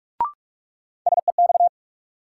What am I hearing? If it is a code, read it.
Morse code, heard as a keyed tone, about 46 wpm, 700 Hz. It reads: SEX
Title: Sentences from Top 1600 Words - New Word Drill - 45wpm